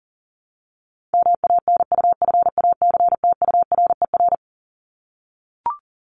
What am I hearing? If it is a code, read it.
Morse code, heard as a keyed tone, about 40 wpm, 700 Hz. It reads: MANUFACTURER
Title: Top 1751-2000 Words 40wpm